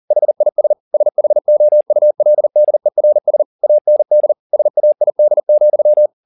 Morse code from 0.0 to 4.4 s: HIS SHOULDERS AND

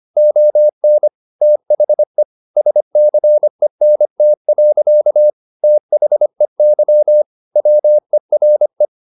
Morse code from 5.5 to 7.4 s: THEY

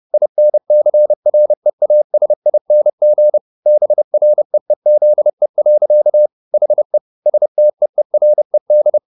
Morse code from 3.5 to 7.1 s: BREEZE. HE

